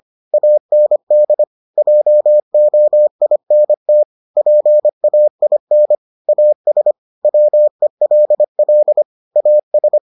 Morse code, heard as a keyed tone, about 25 wpm, 600 Hz. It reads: AND JOINT PAIN AS WELL AS